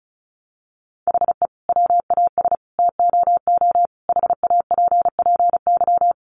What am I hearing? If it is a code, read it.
Morse code, heard as a keyed tone, about 34 wpm, 700 Hz. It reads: HE WAS TOO HAPPY
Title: Sentences from Top 1000 Words 35wpm